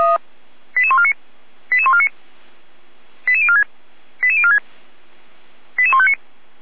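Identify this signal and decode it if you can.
Five-tone selective call (ZVEI), heard as quick runs of five tones, tones of about 70 ms. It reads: 88149 88149 88046 88046 88159
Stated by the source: Five-tone ZVEI selective call received on an SDR, from the Feuerwehr Orlamünde audio server.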